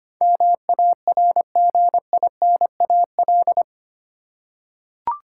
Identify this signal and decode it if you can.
Morse code, heard as a keyed tone, about 25 wpm, 690 Hz. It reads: MARGINAL
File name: 8 Letter Words - First 3 Letters Given - 25wpm